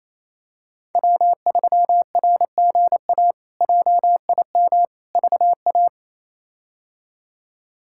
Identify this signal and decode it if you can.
Morse code, heard as a keyed tone, about 28 wpm, 700 Hz. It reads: W3RGA JIM VA
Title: K1USN SST QSO Practice - 28wpm